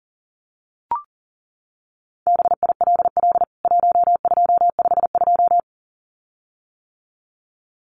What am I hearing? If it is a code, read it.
Morse code, heard as a keyed tone, about 40 wpm, 700 Hz. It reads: BILL 1252
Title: CWT Exchange 40wpm